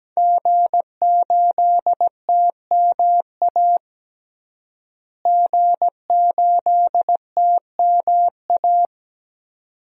Morse code, heard as a keyed tone, about 17 wpm, 700 Hz. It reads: G8TMA G8TMA
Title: DXCC-100 Call Signs - 5 Characters - Repeated 2x - 17wpm